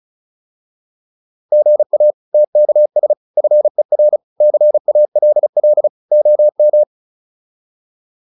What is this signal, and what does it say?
Morse code, heard as a keyed tone, about 34 wpm, 600 Hz. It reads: GA TKS FER CALL OM
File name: QSO409 35